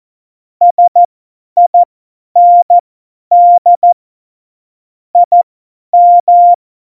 Morse code, heard as a keyed tone, about 14 wpm, spaced as for 10 wpm, 700 Hz. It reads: SIND IM
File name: Morsekurs / Witze / Witz 25. September 2018, 50 F70